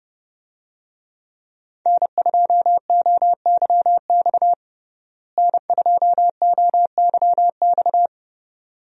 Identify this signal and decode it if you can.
Morse code, heard as a keyed tone, about 30 wpm, 700 Hz. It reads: N2OYX N2OYX
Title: USA Call Signs - 5 Characters - Repeated 2x - 30wpm